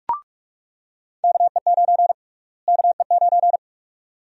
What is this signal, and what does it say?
Morse code, heard as a keyed tone, about 43 wpm, 700 Hz. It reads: KE9 KE9
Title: USA Call Sign Prefixes - Repeated 2x - 45wpm